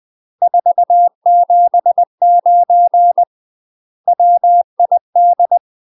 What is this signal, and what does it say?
Morse code, heard as a keyed tone, about 20 wpm, 700 Hz. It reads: 479 WID
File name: UK MASTER QSOs TRAINING / 20WPM / UK MASTER QSOs 0003